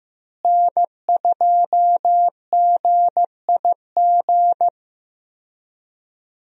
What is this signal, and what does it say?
Morse code, heard as a keyed tone, about 15 wpm, 700 Hz. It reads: N2GIG
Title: USA Call Signs - 5 Characters - Repeated 3x - 15wpm